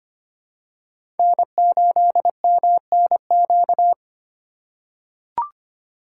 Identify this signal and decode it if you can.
Morse code, heard as a keyed tone, about 25 wpm, 700 Hz. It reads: N8MNQ
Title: USA Call Signs - 3x Character Spacing - 25wpm